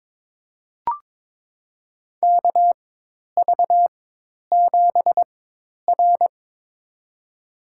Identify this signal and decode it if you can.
Morse code, heard as a keyed tone, about 23 wpm, spaced as for 10 wpm, 700 Hz. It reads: KV7R
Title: USA Call Signs - 4x Character Spacing - 22wpm